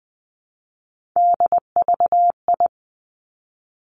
Morse code, heard as a keyed tone, about 20 wpm, 700 Hz. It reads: DVI
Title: Three Challenging Letter-Number Combinations 20wpm